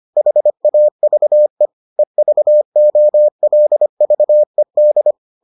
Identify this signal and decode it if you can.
Morse code, heard as a keyed tone, about 25 wpm, 600 Hz. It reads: HAVE EVOLVED